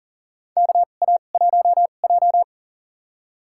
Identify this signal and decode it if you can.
Morse code, heard as a keyed tone, about 40 wpm, spaced as for 29 wpm, 690 Hz. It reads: KA1J
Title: USA Call Signs - 4 Characters - 2x Character Spacing - 40wpm